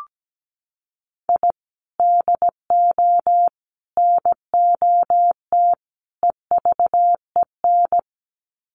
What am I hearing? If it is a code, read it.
Morse code, heard as a keyed tone, about 17 wpm, 700 Hz. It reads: I DO NOT EVEN
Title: Sets of 4 Words 17wpm